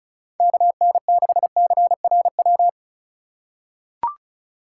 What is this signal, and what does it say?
Morse code, heard as a keyed tone, about 34 wpm, 700 Hz. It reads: KN6CRW
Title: USA Call Signs - 6 Characters - 2x Character Spacing - 35wpm